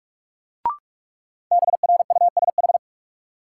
Morse code, heard as a keyed tone, about 47 wpm, 700 Hz. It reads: BRUSH
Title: Sentences from Top 1100 Words - New Word Drill - 45wpm